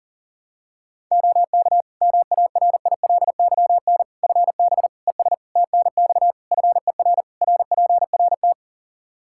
Morse code, heard as a keyed tone, about 40 wpm, 700 Hz. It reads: OK MARILYN FB ES TNX FER RPRT
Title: Practice QSO - Exchange 2 - Info - 40wpm